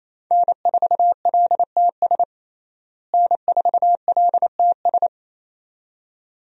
Morse code, heard as a keyed tone, about 28 wpm, 700 Hz. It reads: N4LTS N4LTS
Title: USA Call Signs - 5 Characters - Repeated 2x - 28wpm